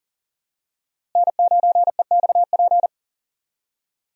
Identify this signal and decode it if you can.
Morse code, heard as a keyed tone, about 39 wpm, 700 Hz. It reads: N9EXP
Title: USA Call Signs - 5 Characters - Repeated 3x - 40wpm